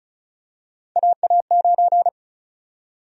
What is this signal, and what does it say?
Morse code, heard as a keyed tone, about 36 wpm, 700 Hz. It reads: AA9
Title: USA Call Sign Prefixes - Repeated 2x - 35wpm